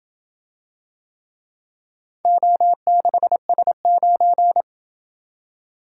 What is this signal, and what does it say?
Morse code, heard as a keyed tone, about 27 wpm, 700 Hz. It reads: O6S9